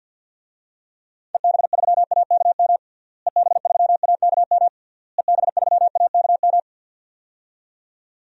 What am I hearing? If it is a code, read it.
Morse code, heard as a keyed tone, about 51 wpm, 700 Hz. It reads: EB3AKM EB3AKM EB3AKM
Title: DXCC-100 Call Signs - 6 Characters - Repeated 3x - 50wpm